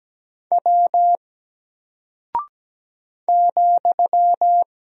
Morse code, heard as a keyed tone, about 17 wpm, 700 Hz. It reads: W ,
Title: Single Character (Letter:Number:Punctuation:Prosign) Alt 17wpm